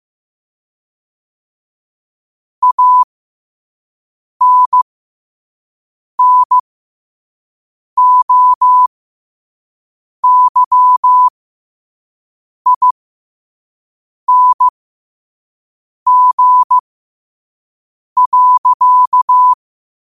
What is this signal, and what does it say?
Morse code, heard as a keyed tone, about 15 wpm, spaced as for 5 wpm, 1000 Hz. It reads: ANNOYING.
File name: qotdfiles / 5wpm